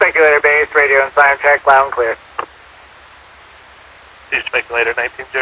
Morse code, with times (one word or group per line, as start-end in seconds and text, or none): none